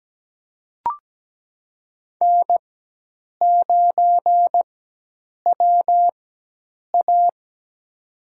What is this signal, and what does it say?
Morse code, heard as a keyed tone, about 17 wpm, spaced as for 8 wpm, 700 Hz. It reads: N9WA
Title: USA Call Signs - 4x Character Spacing - 17wpm